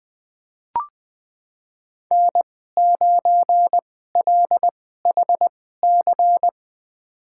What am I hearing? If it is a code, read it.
Morse code, heard as a keyed tone, about 20 wpm, spaced as for 14 wpm, 700 Hz. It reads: N9LHC
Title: USA Call Signs - 2x Character Spacing - 20wpm